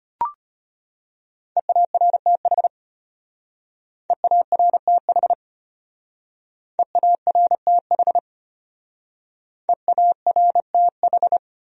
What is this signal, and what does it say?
Morse code, heard as a keyed tone, about 30 wpm, 700 Hz. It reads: EARTH EARTH EARTH EARTH
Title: Speed-Racing Top 300 Words 25wpm